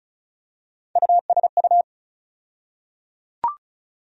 Morse code, heard as a keyed tone, about 36 wpm, 700 Hz. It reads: USU
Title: USA Call Sign Postfixes - 3 Letters 35wpm